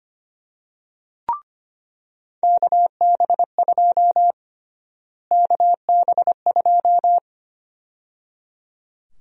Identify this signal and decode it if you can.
Morse code, heard as a keyed tone, about 25 wpm, 700 Hz. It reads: KB2 KB2
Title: USA Call Sign Prefixes - Repeated 2x - 25wpm